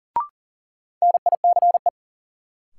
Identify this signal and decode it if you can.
Morse code, heard as a keyed tone, about 40 wpm, 700 Hz. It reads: NICE
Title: Sentences from Top 1100 Words - New Word Drill Alt - 40wpm